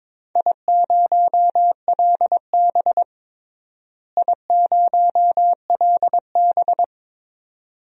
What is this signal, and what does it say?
Morse code, heard as a keyed tone, about 22 wpm, 700 Hz. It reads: I0LB I0LB
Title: DXCC-100 Call Signs - 4 Characters - Repeated 2x - 22wpm